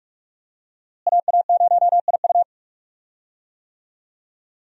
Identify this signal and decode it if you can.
Morse code, heard as a keyed tone, about 43 wpm, 700 Hz. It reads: AA0IU
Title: USA Call Signs - 5 Characters - Repeated 2x - 45wpm